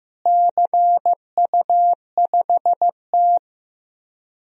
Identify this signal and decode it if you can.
Morse code, heard as a keyed tone, about 15 wpm, 700 Hz. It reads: CU5T
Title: DXCC-100 Call Signs - 4 Characters - Repeated 2x - 15wpm